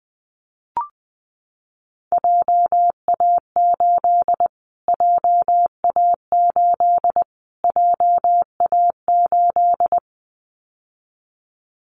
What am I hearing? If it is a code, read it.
Morse code, heard as a keyed tone, about 20 wpm, 700 Hz. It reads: JA8 JA8 JA8
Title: Callsign Prefixes for Top 25 DXCC - Repeated 3x - 20wpm